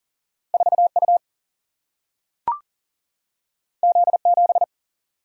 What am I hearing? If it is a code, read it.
Morse code, heard as a keyed tone, about 40 wpm, 700 Hz. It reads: 4U Z7
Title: Single-Letter Single-Number 40wpm